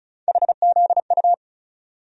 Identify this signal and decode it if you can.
Morse code, heard as a keyed tone, about 36 wpm, 700 Hz. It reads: HZU
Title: USA Call Sign Postfixes - 3 Letters 35wpm